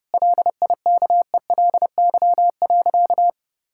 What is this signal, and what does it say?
Morse code, heard as a keyed tone, about 30 wpm, 700 Hz. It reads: LIKELY.
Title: Sentences from Top 1600 Words 30wpm